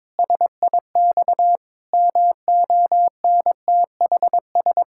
Morse code, heard as a keyed tone, about 22 wpm, 700 Hz. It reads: SIX MONTHS
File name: practice / 22wpm / Sets of 8 Words - 22wpm